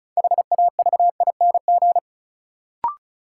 Morse code, heard as a keyed tone, about 36 wpm, 700 Hz. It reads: HAVING
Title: Sets of 5 Words Encore - New Word Drill - 35wpm